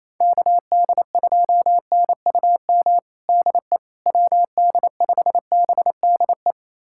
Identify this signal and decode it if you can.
Morse code, heard as a keyed tone, about 28 wpm, 700 Hz. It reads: KD2NUM DE WD5BDE
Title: USA Callsign DE Callsign 28wpm